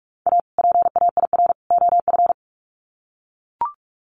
Morse code, heard as a keyed tone, about 44 wpm, 700 Hz. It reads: A PAIR OF
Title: Sets of 3 Words 45wpm